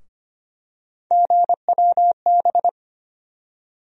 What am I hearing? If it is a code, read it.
Morse code, heard as a keyed tone, about 25 wpm, 700 Hz. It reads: GWB
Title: USA Call Sign Postfixes - 3 Letters 25wpm